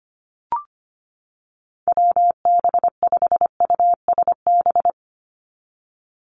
Morse code, heard as a keyed tone, about 25 wpm, 700 Hz. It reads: WB5USB
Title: NA SOTA Activator Callsigns 25wpm